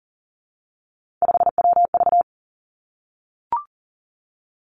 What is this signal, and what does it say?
Morse code, heard as a keyed tone, about 40 wpm, 700 Hz. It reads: 5WV
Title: Three Challenging Letter-Number Combinations 40wpm